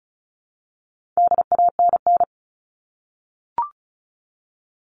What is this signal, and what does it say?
Morse code, heard as a keyed tone, about 34 wpm, 700 Hz. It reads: DANN
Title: All Cut Numbers - 4-Digit 35wpm